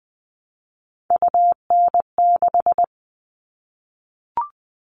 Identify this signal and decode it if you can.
Morse code, heard as a keyed tone, about 20 wpm, 700 Hz. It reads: UN6
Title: All Cut Numbers - 3-Digit 20wpm